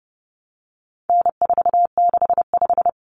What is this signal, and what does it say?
Morse code, heard as a keyed tone, about 30 wpm, 700 Hz. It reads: N465
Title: Common Cut Numbers - 4-Digit 30wpm